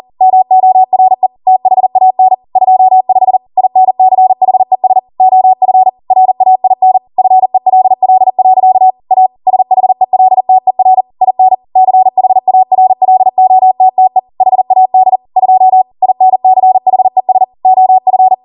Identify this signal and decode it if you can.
Morse code, heard as a keyed tone, about 40 wpm, 750 Hz. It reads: MORE THAN 25 INCHES OF RAIN FELL. A SHELTER IN CHARLOTTE HAD 2 INCHES OF